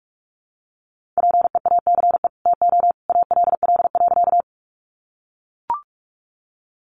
Morse code, heard as a keyed tone, about 44 wpm, 700 Hz. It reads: PEACE TO ALL.
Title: Sentences from Top 1500 Words 45wpm